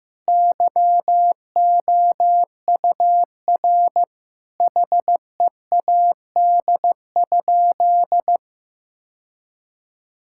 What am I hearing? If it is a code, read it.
Morse code, heard as a keyed tone, about 15 wpm, 700 Hz. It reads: YOUR HEAD?
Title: Sentences from Top 1000 Words - Review Part 5 - 15wpm